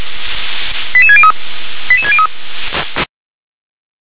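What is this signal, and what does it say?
Five-tone selective call (ZVEI), heard as quick runs of five tones, tones of about 70 ms. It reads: 88603 88603
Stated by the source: ZVEI five-tone selective call received on an SDR.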